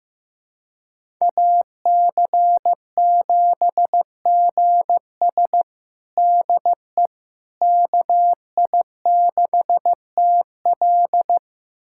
Morse code, heard as a keyed tone, about 15 wpm, 700 Hz. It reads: AC7GS DE KI6TL
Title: USA Callsign DE Callsign - 5 Characters - 15wpm